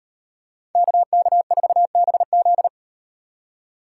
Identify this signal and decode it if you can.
Morse code, heard as a keyed tone, about 38 wpm, 700 Hz. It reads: KK4BZ